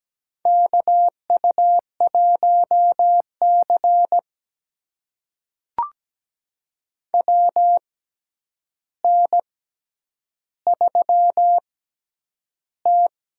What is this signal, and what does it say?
Morse code, heard as a keyed tone, about 17 wpm, spaced as for 9 wpm, 700 Hz. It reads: KU1C WN3T
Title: USA Call Signs - 4 Characters - 6x Character Spacing - 17wpm